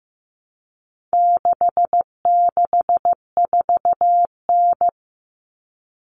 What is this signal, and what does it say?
Morse code, heard as a keyed tone, about 15 wpm, 700 Hz. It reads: 664N
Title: Common Cut Numbers - 4-Digit 15wpm